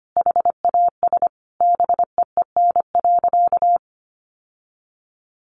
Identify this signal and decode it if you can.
Morse code, heard as a keyed tone, about 25 wpm, 700 Hz. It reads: HAS BEEN.
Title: Sentences from Top 1400 Words 25wpm